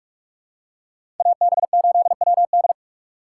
Morse code, heard as a keyed tone, about 42 wpm, 700 Hz. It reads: AB8WD
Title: USA Call Signs - 6x Character Spacing - 45wpm